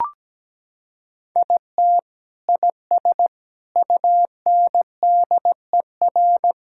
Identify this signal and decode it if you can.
Morse code, heard as a keyed tone, about 17 wpm, 700 Hz. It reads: IT IS UNDER
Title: Sentences from Top 1000 Words - Review Part 6 - 17wpm